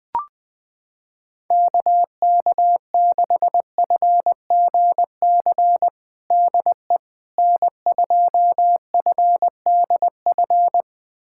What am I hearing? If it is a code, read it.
Morse code, heard as a keyed tone, about 20 wpm, 700 Hz. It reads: KK6FGC DE N2FDF
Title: USA Callsign DE Callsign 20wpm